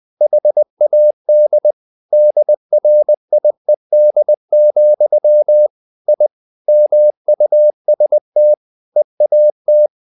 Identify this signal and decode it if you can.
Morse code, heard as a keyed tone, about 20 wpm, 600 Hz. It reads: HAD DRIED, I MUST EAT
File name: the old man and the sea / the old man and the sea part 1 20wpm